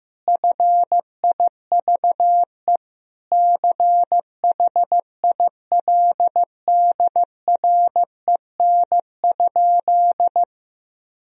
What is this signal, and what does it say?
Morse code, heard as a keyed tone, about 15 wpm, 700 Hz. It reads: FIVE CHILDREN?
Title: Sentences from Top 1000 Words - Review Part 2 - 15wpm